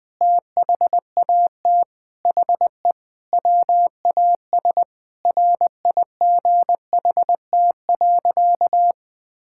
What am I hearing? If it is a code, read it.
Morse code, heard as a keyed tone, about 20 wpm, 700 Hz. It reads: THAT HE WAS RIGHT.
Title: Sentences with 5-Word Phrases - 20wpm